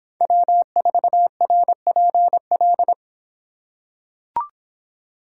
Morse code, heard as a keyed tone, about 25 wpm, 700 Hz. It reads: W4RPL